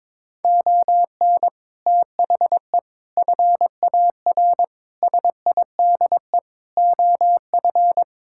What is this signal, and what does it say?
Morse code, heard as a keyed tone, about 22 wpm, 700 Hz. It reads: ON THE FAR SIDE OF
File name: practice / 22wpm / Build A Phrase - 5-Word Phrases - 22wpm